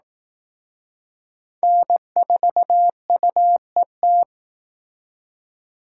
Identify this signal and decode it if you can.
Morse code, heard as a keyed tone, about 19 wpm, 700 Hz. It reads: N4UET